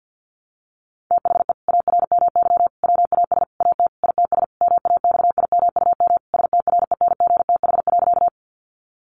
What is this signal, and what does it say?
Morse code, heard as a keyed tone, about 50 wpm, 700 Hz. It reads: THE ARMY WAS AT ITS MAXIMUM STRENGTH.
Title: Sentences from Top 1300 Words 50wpm